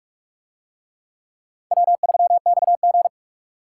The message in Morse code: W3XG

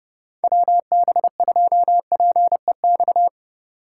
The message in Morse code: WB2PEX

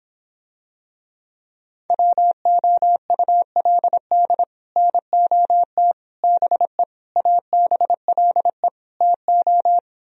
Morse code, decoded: WOULD NOT BE ABLE TO

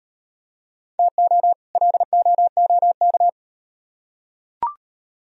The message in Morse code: TO LOOK

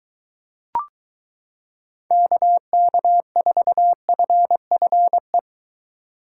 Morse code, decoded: KK4FFE

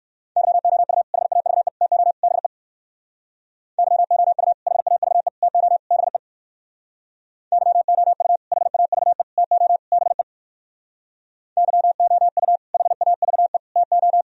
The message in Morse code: YOU HAVE TO BE YOU HAVE TO BE YOU HAVE TO BE YOU HAVE TO

